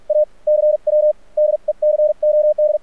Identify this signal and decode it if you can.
Morse code, heard as a keyed tone, about 48 wpm, spaced as for 40 wpm, 600 Hz. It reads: A CW DECOD
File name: archived / CRSnet